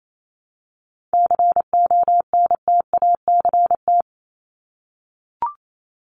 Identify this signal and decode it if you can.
Morse code, heard as a keyed tone, about 28 wpm, 700 Hz. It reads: CONTACT
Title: Top 751-1000 Words 28wpm